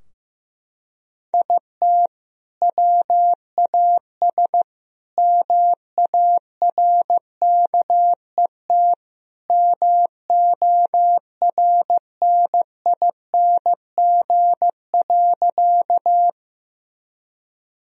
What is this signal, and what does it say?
Morse code, heard as a keyed tone, about 14 wpm, 700 Hz. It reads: IT WAS MARKET MORNING.
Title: Sentences from Top 1000 Words - Review Part 3 - 15wpm